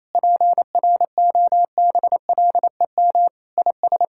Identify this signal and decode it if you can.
Morse code, heard as a keyed tone, about 28 wpm, 700 Hz. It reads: PROBLEM IS
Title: Sentences from Top 1600 Words 28wpm